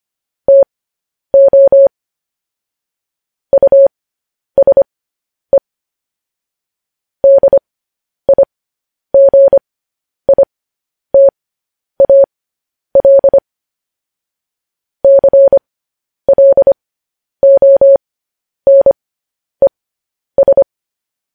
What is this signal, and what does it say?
Morse code, heard as a keyed tone, about 25 wpm, spaced as for 10 wpm, 550 Hz. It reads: TO USE DIGITAL CLONES
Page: news-headlines-in-morse-code-at-10-wpm